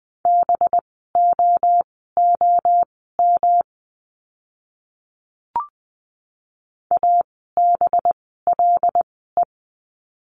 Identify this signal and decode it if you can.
Morse code, heard as a keyed tone, about 20 wpm, spaced as for 15 wpm, 700 Hz. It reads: BOOM ABLE
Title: 185 Lesson 30f - Letter b - Words 20wpm